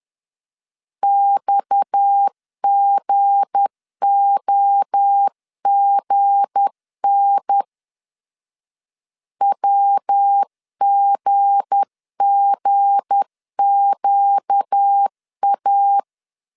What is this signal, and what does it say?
Morse code, heard as a keyed tone, about 10 wpm, 800 Hz. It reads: XGOGN WGGQA
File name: LEZIONI / 7 GF9Z / 7 G 10wpm 2501